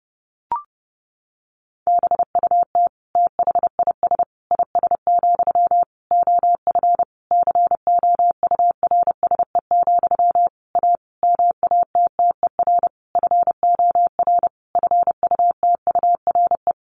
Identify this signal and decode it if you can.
Morse code, heard as a keyed tone, about 28 wpm, 700 Hz. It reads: BUT THIS IS, OF COURSE, A MATTER FOR FUTURE
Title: Sentences from Top 1500 Words 30wpm